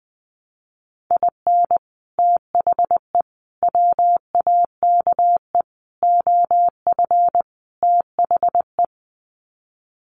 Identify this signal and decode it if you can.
Morse code, heard as a keyed tone, about 20 wpm, 700 Hz. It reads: IN THE WAKE OF THE
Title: Sets of 5 Words 20wpm